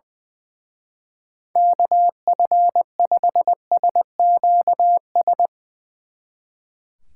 Morse code, heard as a keyed tone, about 20 wpm, 700 Hz. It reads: KF5SQS
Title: Speed Racing - USA 2x3 Call Signs 20wpm